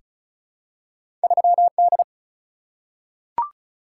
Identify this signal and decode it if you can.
Morse code, heard as a keyed tone, about 36 wpm, 700 Hz. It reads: 3 D
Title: Single-Letter Single-Number 35wpm